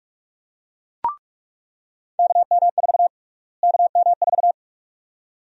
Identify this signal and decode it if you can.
Morse code, heard as a keyed tone, about 41 wpm, 700 Hz. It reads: KM4 KM4
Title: USA Call Sign Prefixes - Repeated 2x - 45wpm